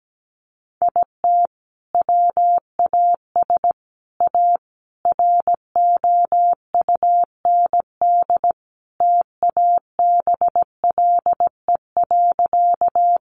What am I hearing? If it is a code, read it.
Morse code, heard as a keyed tone, about 17 wpm, 700 Hz. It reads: IT WAS A ROUND TABLE.
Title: Sentences from Top 400 Words 17wpm